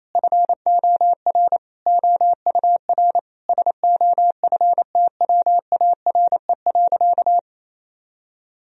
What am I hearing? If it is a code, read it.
Morse code, heard as a keyed tone, about 28 wpm, 700 Hz. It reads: FOR OUR SOFTWARE.
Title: Sentences from Top 1000 Words - Review Part 3 - 28wpm